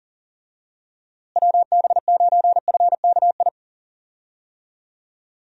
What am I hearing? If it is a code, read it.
Morse code, heard as a keyed tone, about 40 wpm, 700 Hz. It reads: WB9FKI
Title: USA Call Signs - 6 Characters - Repeated 3x - 40wpm